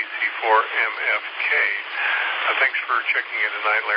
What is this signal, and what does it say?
Emergency Alert System (SAME) tones, heard as harsh buzzing bursts, with speech. It is heard from 0.7 to 3.5 s.